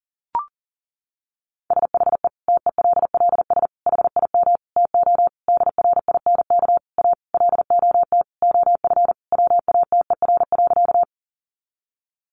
Morse code, heard as a keyed tone, about 37 wpm, 700 Hz. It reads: SHE TELLS HIM TO DRINK A LOT OF WATER.
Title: Additional Easy Phrases 40wpm